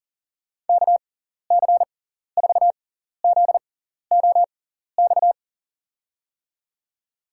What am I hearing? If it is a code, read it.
Morse code, heard as a keyed tone, about 40 wpm, spaced as for 14 wpm, 700 Hz. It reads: KC4ZOX
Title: USA Call Signs - 6x Character Spacing - 40wpm